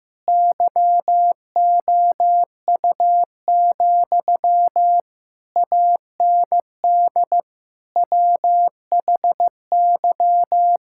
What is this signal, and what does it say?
Morse code, heard as a keyed tone, about 16 wpm, 700 Hz. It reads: YOU, AND WHY
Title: Sentences from Top 1600 Words 15wpm